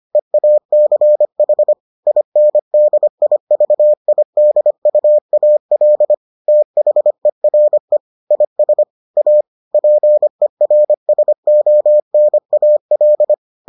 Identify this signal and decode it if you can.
Morse code, heard as a keyed tone, about 25 wpm, 600 Hz. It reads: EACH INDIVIDUAL THERE IS A PERSONAL